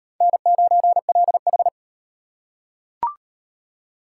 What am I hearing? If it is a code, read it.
Morse code, heard as a keyed tone, about 35 wpm, 700 Hz. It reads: N9LH